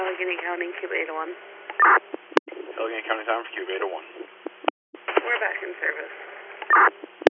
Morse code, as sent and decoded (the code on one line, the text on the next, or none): none